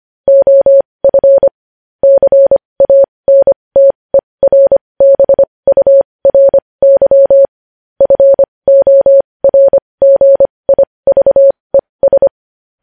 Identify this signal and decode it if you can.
Morse code, heard as a keyed tone, about 25 wpm, spaced as for 20 wpm, 550 Hz. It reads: OF CANTERBURY FORGIVES
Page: news headlines in morse code at 20 wpm